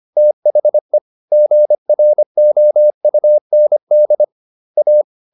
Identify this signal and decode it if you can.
Morse code, heard as a keyed tone, about 25 wpm, 600 Hz. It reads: THE GROUND A
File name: SS202 25